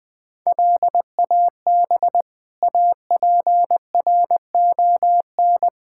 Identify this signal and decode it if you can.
Morse code, heard as a keyed tone, about 20 wpm, 700 Hz. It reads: LAB APRON
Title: Difficult Words - Speed-Racing with Hints - 20wpm